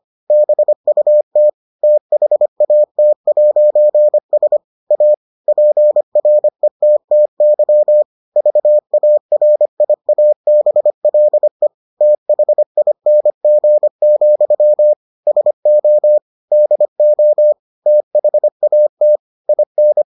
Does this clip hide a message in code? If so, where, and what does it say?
Yes: Morse code from 0.3 to 20.0 s, heard as a keyed tone, about 25 wpm, 600 Hz. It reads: BUT THAT'S A PRETTY VARIABLE THING, SO DO THAT IN